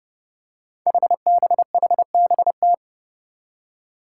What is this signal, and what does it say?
Morse code, heard as a keyed tone, about 30 wpm, 700 Hz. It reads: HBHBT
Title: Five Challenging Letter-Number Combinations 30wpm